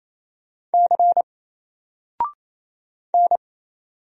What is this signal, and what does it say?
Morse code, heard as a keyed tone, about 27 wpm, 700 Hz. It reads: C N
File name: Single-Letter-Number-28wpm